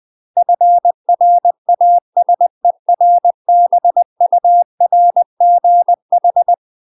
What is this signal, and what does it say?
Morse code, heard as a keyed tone, about 20 wpm, 700 Hz. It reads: FRASERBURGH
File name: UK MASTER QSOs TRAINING / 20WPM / UK MASTER QSOs 0048